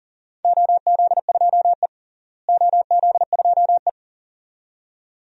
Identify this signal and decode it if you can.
Morse code, heard as a keyed tone, about 40 wpm, 700 Hz. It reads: OZ2E OZ2E
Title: DXCC-100 Call Signs - 4 Characters - Repeated 2x - 40wpm